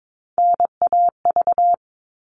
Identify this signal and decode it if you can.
Morse code, heard as a keyed tone, about 22 wpm, 700 Hz. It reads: NAV
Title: All Cut Numbers - 3-Digit 22wpm